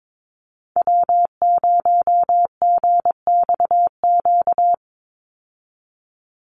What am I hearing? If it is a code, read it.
Morse code, heard as a keyed tone, about 22 wpm, 700 Hz. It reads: W0GXQ
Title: Call Signs Repeated 3x - 22wpm